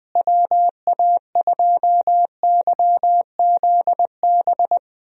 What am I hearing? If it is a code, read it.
Morse code, heard as a keyed tone, about 20 wpm, 700 Hz. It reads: WA2YZB
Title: USA Call Signs - 6 Characters - Repeated 3x - 20wpm